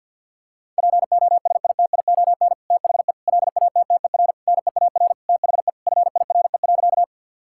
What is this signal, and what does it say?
Morse code, heard as a keyed tone, about 50 wpm, 700 Hz. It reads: POSITION THE LATTER NEAR THE FIRE.
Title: Sentences from Top 1600 Words 50wpm